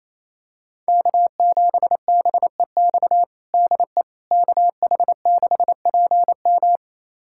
Morse code, heard as a keyed tone, about 28 wpm, 700 Hz. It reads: K7BEX DE KH6PM
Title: USA Callsign DE Callsign - 5 Characters - 28wpm